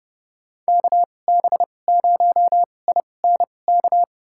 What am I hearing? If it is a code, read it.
Morse code, heard as a keyed tone, about 30 wpm, spaced as for 22 wpm, 700 Hz. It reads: KB0INK
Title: USA Call Signs - 6 Characters - 2x Character Spacing - 30wpm